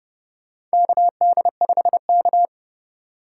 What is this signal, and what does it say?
Morse code, heard as a keyed tone, about 29 wpm, 700 Hz. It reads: KD5K
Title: USA Call Signs - 4 Characters - 4x Character Spacing - 30wpm